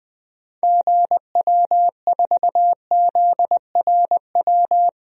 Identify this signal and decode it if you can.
Morse code, heard as a keyed tone, about 21 wpm, 700 Hz. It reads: GW4ZRW
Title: DXCC-100 Call Signs - 6 Characters - 20wpm